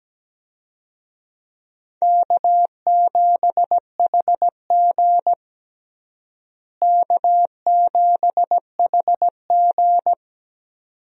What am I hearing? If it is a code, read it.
Morse code, heard as a keyed tone, about 17 wpm, 700 Hz. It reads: K7HG K7HG
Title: USA Call Signs - 4 Characters - Repeated 2x - 17wpm